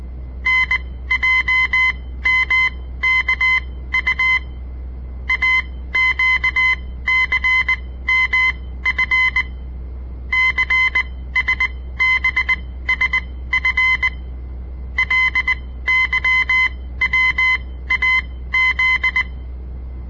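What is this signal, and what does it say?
Morse code, heard as a keyed tone, about 19 wpm, spaced as for 14 wpm, 2050 Hz. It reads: NJMKU AQCMF CSBSF LYWAZ
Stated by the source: On-air Morse code practice broadcast by a southern Montana amateur radio association; all code, no read back.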